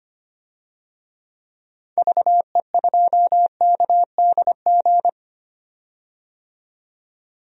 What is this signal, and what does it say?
Morse code, heard as a keyed tone, about 25 wpm, 700 Hz. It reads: VE2KDG
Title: DXCC-100 Call Signs - 6 Characters - 25wpm